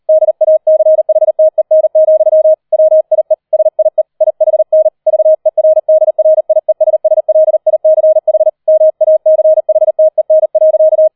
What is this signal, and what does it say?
Morse code, heard as a keyed tone, about 38 wpm, 630 Hz. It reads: DACHTEN, WIE SIE IHN VERDRIESSLICH MACHTEN.
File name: 003 Dritter Streich 40wpm